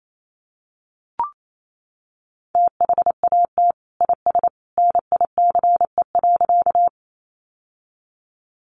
Morse code, heard as a keyed tone, about 28 wpm, 700 Hz. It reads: THAT IS NICE.
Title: Sentences from Top 1100 Words 28wpm